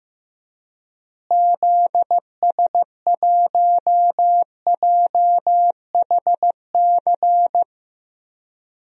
Morse code, heard as a keyed tone, about 15 wpm, 700 Hz. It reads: ZS1JHC
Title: DXCC-100 Call Signs - 6 Characters - Repeated 3x - 15wpm